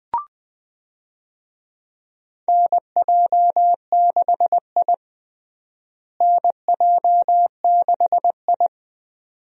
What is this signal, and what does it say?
Morse code, heard as a keyed tone, about 20 wpm, 700 Hz. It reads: NJ6I NJ6I
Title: USA Call Signs - 4 Characters - Repeated 2x - 20wpm